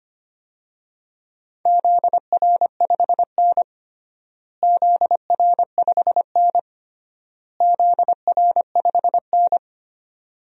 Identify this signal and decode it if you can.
Morse code, heard as a keyed tone, about 25 wpm, 700 Hz. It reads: ZR5N ZR5N ZR5N
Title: DXCC-100 Call Signs - 4 Characters - Repeated 3x - 25wpm